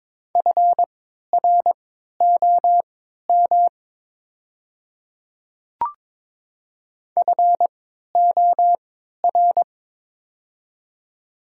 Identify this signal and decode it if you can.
Morse code, heard as a keyed tone, about 22 wpm, spaced as for 12 wpm, 700 Hz. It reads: FROM FOR
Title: 3x Character Spacing - Top 100 Words - 22wpm